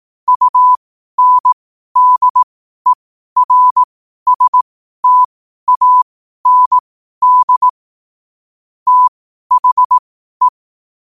Morse code, heard as a keyed tone, about 18 wpm, spaced as for 12 wpm, 1000 Hz. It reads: UNDERSTAND THE